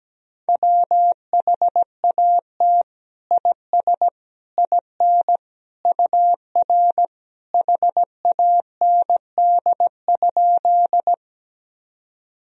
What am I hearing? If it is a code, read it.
Morse code, heard as a keyed tone, about 17 wpm, 700 Hz. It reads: WHAT IS IN UR HAND?
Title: Sentences from Top 200 Words and Abbreviations 17wpm